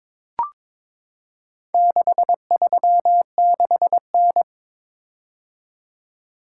Morse code, heard as a keyed tone, about 22 wpm, 700 Hz. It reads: 636N
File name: practice / 22wpm / Four Letter-Number Combinations - 22wpm